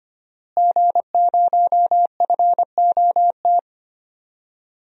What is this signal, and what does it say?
Morse code, heard as a keyed tone, about 25 wpm, 700 Hz. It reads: G0FOT
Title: DXCC-100 Call Signs - 5 Characters - Repeated 3x - 25wpm